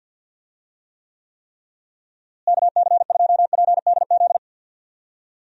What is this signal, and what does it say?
Morse code, heard as a keyed tone, about 48 wpm, 700 Hz. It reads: KC2PDZ
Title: USA Call Signs - 6 Characters - Repeated 3x - 50wpm